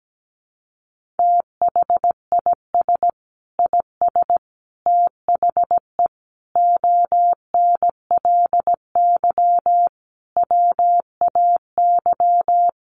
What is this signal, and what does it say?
Morse code, heard as a keyed tone, about 17 wpm, 700 Hz. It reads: THIS IS THE ONLY WAY
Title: Sentences from Top 1400 Words 17wpm